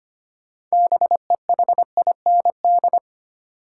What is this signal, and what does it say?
Morse code, heard as a keyed tone, about 25 wpm, 700 Hz. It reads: BEHIND
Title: Six Letter Words - 25wpm